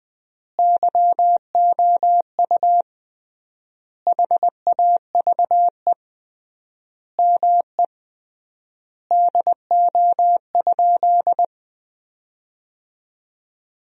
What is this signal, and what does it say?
Morse code, heard as a keyed tone, about 20 wpm, 700 Hz. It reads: YOU HAVE ME DO?
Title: 3x Word-Spacing - Sentences from Top 200 Words - 20wpm